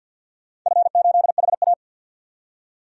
Morse code, heard as a keyed tone, about 49 wpm, 700 Hz. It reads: F8HA
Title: DXCC-100 Call Signs - 4 Characters - 50wpm